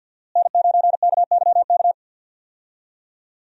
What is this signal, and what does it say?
Morse code, heard as a keyed tone, about 52 wpm, 700 Hz. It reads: N9KYK
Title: USA Call Signs - 5 Characters - Repeated 2x - 50wpm